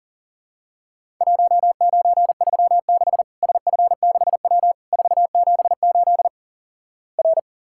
Morse code, heard as a keyed tone, about 37 wpm, 700 Hz. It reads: 1936 SF6W 478 R
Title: Worked All Europe DX Contest - QTC Exchange - 40wpm